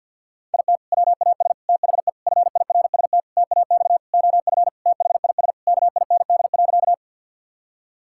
Morse code, heard as a keyed tone, about 50 wpm, 700 Hz. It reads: IT WAS THE FIRST TAX OF THIS KIND.
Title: Sentences from Top 1000 Words - Review Part 5 - 50wpm